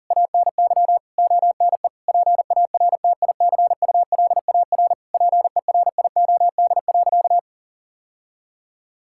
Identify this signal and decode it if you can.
Morse code, heard as a keyed tone, about 40 wpm, 700 Hz. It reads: ANY ONE PARTICULAR PERIOD.
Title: Sentences from Top 1000 Words - Review Part 5 - 40wpm